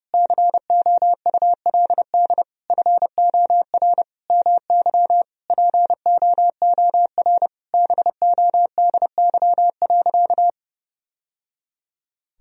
Morse code, heard as a keyed tone, about 32 wpm, 700 Hz. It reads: COULD FOR MY POOR BODY.
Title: Sentences from Top 1000 Words - Review Part 2 - 30wpm